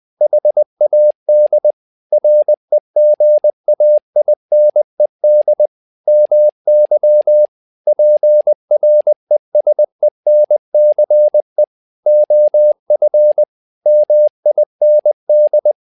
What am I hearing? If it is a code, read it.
Morse code, heard as a keyed tone, about 20 wpm, 600 Hz. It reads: HAD REGAINED MY PRESENCE OF MIND
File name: tower trouble 20wpm